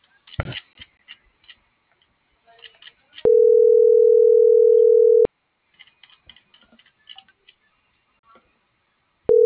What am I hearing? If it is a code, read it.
Ringback tone (North America), 440 + 480 Hz.